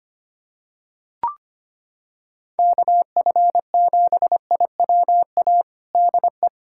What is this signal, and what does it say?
Morse code, heard as a keyed tone, about 25 wpm, 700 Hz. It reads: KF7IWA DE
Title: USA Callsign DE Callsign 25wpm